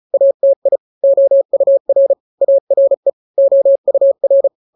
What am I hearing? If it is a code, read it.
Morse code, heard as a keyed tone, about 34 wpm, 550 Hz. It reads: ATI OUR ARE OUR
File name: 3-grams 35 30 WPM